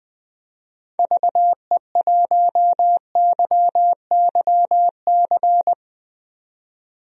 Morse code, heard as a keyed tone, about 20 wpm, 700 Hz. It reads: VE1YYC
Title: DXCC-100 Call Signs - 6 Characters - 20wpm